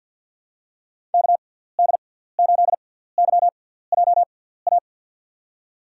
Morse code, heard as a keyed tone, about 53 wpm, spaced as for 17 wpm, 700 Hz. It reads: KD8YJA
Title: USA Call Signs - 6x Character Spacing - 50wpm